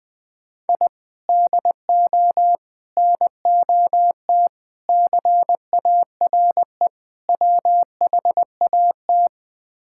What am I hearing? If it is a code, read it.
Morse code, heard as a keyed tone, about 20 wpm, 700 Hz. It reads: I DO NOT CARE WHAT
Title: Build A Phrase - 5-Word Phrases - 20wpm